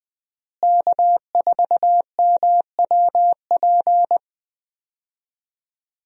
Morse code, heard as a keyed tone, about 20 wpm, 700 Hz. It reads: K4MWP
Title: USA Call Signs - 5 Characters - Repeated 3x - 20wpm